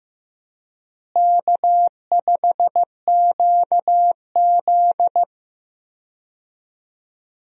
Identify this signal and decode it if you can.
Morse code, heard as a keyed tone, about 15 wpm, 700 Hz. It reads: K5QZ
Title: USA Call Signs - 4 Characters - Repeated 2x - 15wpm